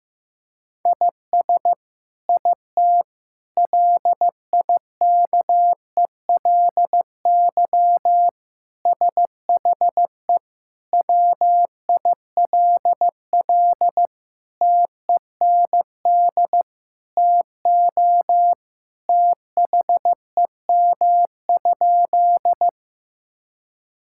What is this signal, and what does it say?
Morse code, heard as a keyed tone, about 15 wpm, 700 Hz. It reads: IS IT LIKELY SHE WILL TEND TO THEM?
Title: Sentences from Top 1000 Words - Review Part 5 - 15wpm